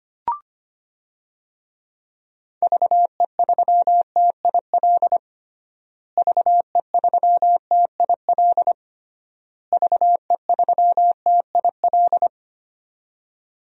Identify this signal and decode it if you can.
Morse code, heard as a keyed tone, about 25 wpm, 700 Hz. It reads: VE3TIL VE3TIL VE3TIL
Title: DXCC-100 Call Signs - 6 Characters - Repeated 3x - 25wpm